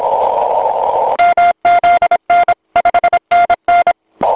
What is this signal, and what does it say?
Morse code, heard as a keyed tone, about 26 wpm, 1400 Hz, with noise. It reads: MZN 5NN
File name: voice / 80m j3 pa5et 002 2001 08 08